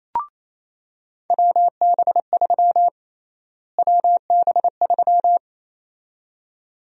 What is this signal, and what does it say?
Morse code, heard as a keyed tone, about 28 wpm, 700 Hz. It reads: WB3 WB3
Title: USA Call Sign Prefixes - Repeated 2x - 28wpm